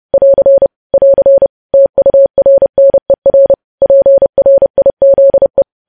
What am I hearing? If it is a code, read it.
Morse code, heard as a keyed tone, about 30 wpm, 550 Hz. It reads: <AR> <AR> TURNER PRIZE